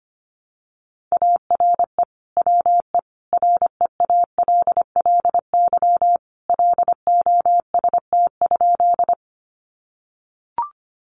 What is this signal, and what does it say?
Morse code, heard as a keyed tone, about 25 wpm, 700 Hz. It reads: ARE WE REALLY LOST?
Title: Sentences from Top 1400 Words 25wpm